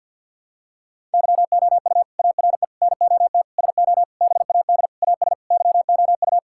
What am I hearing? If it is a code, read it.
Morse code, heard as a keyed tone, about 52 wpm, 700 Hz. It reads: YOU ARE NOT SO BAD AS YOU
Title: Sentences from Top 1000 Words - Review Part 6 - 50wpm